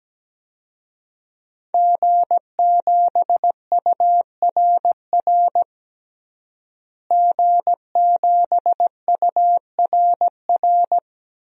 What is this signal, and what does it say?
Morse code, heard as a keyed tone, about 17 wpm, 700 Hz. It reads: G7URR G7URR
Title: DXCC-100 Call Signs - 5 Characters - Repeated 3x - 17wpm